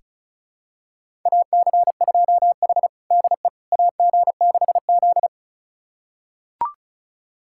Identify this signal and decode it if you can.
Morse code, heard as a keyed tone, about 34 wpm, 700 Hz. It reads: AC2H DE AG6Z
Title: USA Callsign DE Callsign - 4 Characters - 35wpm